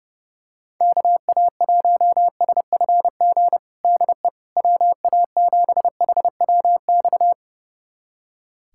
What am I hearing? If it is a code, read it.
Morse code, heard as a keyed tone, about 32 wpm, 700 Hz. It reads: KA1SFG DE WA7HWX
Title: USA Callsign DE Callsign 30wpm